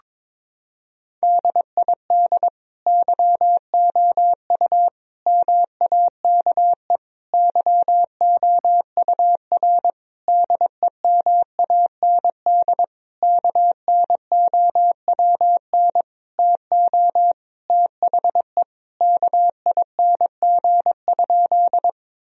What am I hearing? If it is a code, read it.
Morse code, heard as a keyed tone, about 22 wpm, 700 Hz. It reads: DID YOU MAKE YOUR DEMAND KNOWN TO THE KING?
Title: Sentences from Top 1000 Words - Review Part 4 - 22wpm